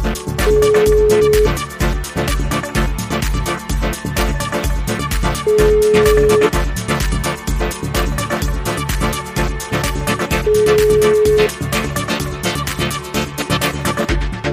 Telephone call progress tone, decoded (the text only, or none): ringback tone